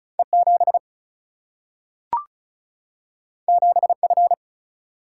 Morse code, heard as a keyed tone, about 36 wpm, 700 Hz. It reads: E7 7F
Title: Single-Letter Single-Number 35wpm